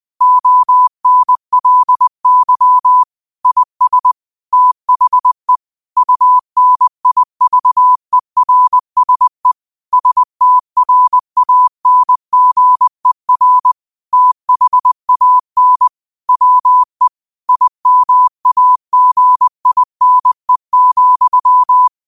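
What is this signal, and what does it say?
Morse code, heard as a keyed tone, about 20 wpm, 1000 Hz. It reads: ONLY IS THE UNIVERSE STRANGER THAN WE IMAGINE,
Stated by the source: Morse code practice at 20 wpm: famous quotations sent in telegraphy, from URVAG.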